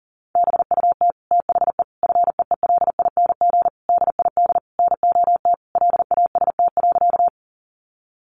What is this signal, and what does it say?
Morse code, heard as a keyed tone, about 40 wpm, 700 Hz. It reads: BUT THE FEELING DID NOT LAST.
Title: Sentences from Top 1200 Words 40wpm